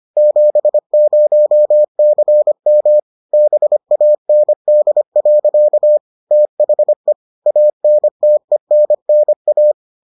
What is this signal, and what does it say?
Morse code, heard as a keyed tone, about 25 wpm, 600 Hz. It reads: 70CM BAND. THE ANTENNA